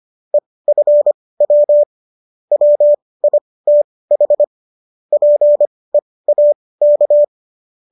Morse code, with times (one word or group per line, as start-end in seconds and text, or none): 0.3-1.8 EFW
2.5-4.4 WITH
5.1-7.2 PEAK